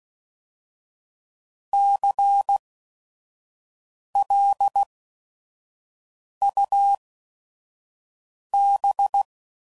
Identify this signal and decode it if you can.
Morse code, heard as a keyed tone, about 16 wpm, spaced as for 5 wpm, 800 Hz. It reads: CLUB
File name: Prefixes / By First Letter / Cs